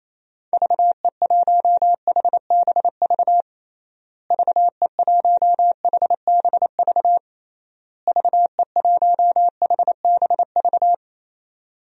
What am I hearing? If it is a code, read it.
Morse code, heard as a keyed tone, about 28 wpm, 700 Hz. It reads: VE1HBV VE1HBV VE1HBV
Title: DXCC-100 Call Signs - 6 Characters - Repeated 3x - 28wpm